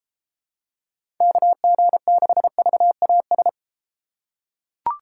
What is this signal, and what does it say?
Morse code, heard as a keyed tone, about 31 wpm, 700 Hz. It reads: KG6VAS